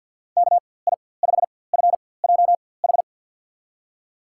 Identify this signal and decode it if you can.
Morse code, heard as a keyed tone, about 51 wpm, spaced as for 23 wpm, 700 Hz. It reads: KI5FJH
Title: USA Call Signs - 6 Characters - 4x Character Spacing - 50wpm